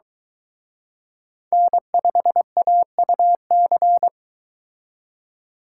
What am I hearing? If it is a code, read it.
Morse code, heard as a keyed tone, about 23 wpm, 700 Hz. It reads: N5AUC